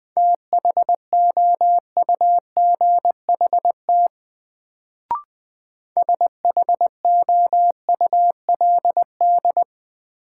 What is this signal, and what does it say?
Morse code, heard as a keyed tone, about 20 wpm, 700 Hz. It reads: THOUGHT SHOULD
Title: Sentences from Top 300 Words - New Word Drill - 20wpm